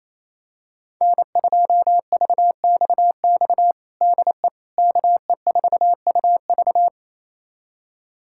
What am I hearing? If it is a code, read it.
Morse code, heard as a keyed tone, about 28 wpm, 700 Hz. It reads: N2VXX DE KE4UV
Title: USA Callsign DE Callsign - 5 Characters - 28wpm